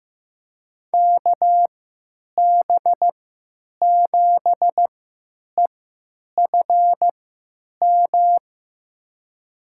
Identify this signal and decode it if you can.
Morse code, heard as a keyed tone, about 15 wpm, spaced as for 9 wpm, 700 Hz. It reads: KB7EFM
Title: USA Call Signs - 3x Character Spacing - 15wpm